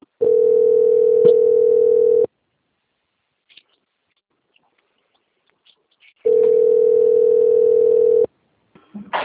Ringback tone (North America), 440 + 480 Hz.
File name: out- - - - - .8422